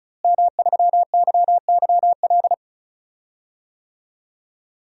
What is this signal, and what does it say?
Morse code, heard as a keyed tone, about 34 wpm, 700 Hz. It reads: M3YYL
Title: DXCC-100 Call Signs - 5 Characters - Repeated 2x - 35wpm